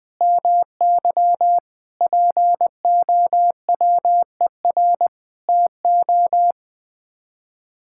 Morse code, heard as a keyed tone, about 20 wpm, 700 Hz. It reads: MY POWER TO
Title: Sets of 8 Words - 20wpm